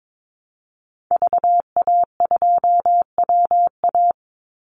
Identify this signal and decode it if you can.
Morse code, heard as a keyed tone, about 22 wpm, 700 Hz. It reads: VA2WA
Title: Call Signs Repeated 3x - 22wpm